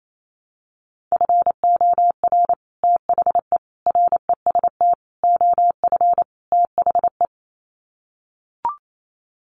Morse code, heard as a keyed tone, about 28 wpm, 700 Hz. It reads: FOR THE REST OF THE
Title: Sets of 5 Words 28wpm